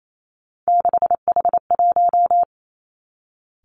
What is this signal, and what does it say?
Morse code, heard as a keyed tone, about 28 wpm, 700 Hz. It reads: 6H1
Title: Three Challenging Letter-Number Combinations 28wpm